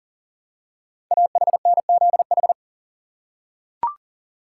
Morse code, heard as a keyed tone, about 40 wpm, 700 Hz. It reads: AHNZH
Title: Code Groups - 5 Letters - 40wpm